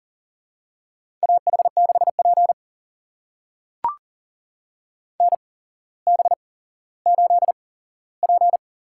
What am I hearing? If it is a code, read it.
Morse code, heard as a keyed tone, about 37 wpm, spaced as for 17 wpm, 700 Hz. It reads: AH6P NB8P
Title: USA Call Signs - 4 Characters - 8x Character Spacing - 40wpm